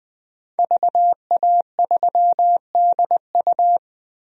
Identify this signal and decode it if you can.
Morse code, heard as a keyed tone, about 21 wpm, 700 Hz. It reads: VA3DU
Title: DXCC-100 Call Signs - 5 Characters - 20wpm